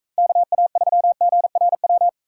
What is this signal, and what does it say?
Morse code, heard as a keyed tone, about 42 wpm, 700 Hz. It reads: KA3GRW